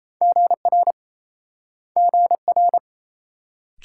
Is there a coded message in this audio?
Yes: Morse code from 0.2 to 2.8 s, heard as a keyed tone, about 38 wpm, 700 Hz. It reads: GR GR